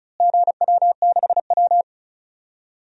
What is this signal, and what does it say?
Morse code, heard as a keyed tone, about 36 wpm, 700 Hz. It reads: GW6W
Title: DXCC-100 Call Signs - 4 Characters - Repeated 2x - 35wpm